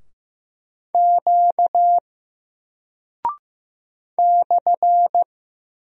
Morse code, heard as a keyed tone, about 15 wpm, 700 Hz. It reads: Q /
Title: Single Character (Letter:Number:Punctuation:Prosign) 15wpm